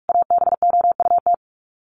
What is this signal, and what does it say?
Morse code, heard as a keyed tone, about 44 wpm, 700 Hz. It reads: ABOUT
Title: Sets of 5 Words 45wpm